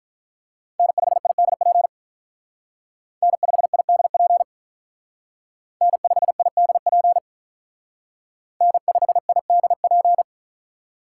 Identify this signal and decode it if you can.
Morse code, heard as a keyed tone, about 43 wpm, 700 Hz. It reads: N5IDP N5IDP N5IDP N5IDP
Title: Speed Racing - USA 1x3 Call Signs 35wpm